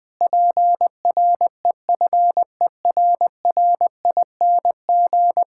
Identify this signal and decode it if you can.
Morse code, heard as a keyed tone, about 20 wpm, 700 Hz. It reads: PREFERRING
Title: Pangram Challenge v2 - 20wpm